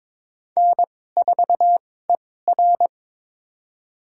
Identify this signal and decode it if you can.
Morse code, heard as a keyed tone, about 21 wpm, spaced as for 16 wpm, 700 Hz. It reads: N4ER
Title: USA Call Signs - 4 Characters - 2x Character Spacing - 22wpm